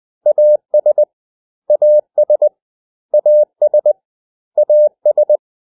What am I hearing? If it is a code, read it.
Morse code, heard as a keyed tone, about 20 wpm, 600 Hz. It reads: AS AS AS AS
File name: cw / 100 Common English Words